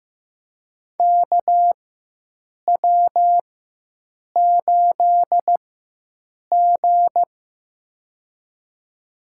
Morse code, heard as a keyed tone, about 16 wpm, spaced as for 7 wpm, 700 Hz. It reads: KW8G
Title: USA Call Signs - 4 Characters - 4x Character Spacing - 15wpm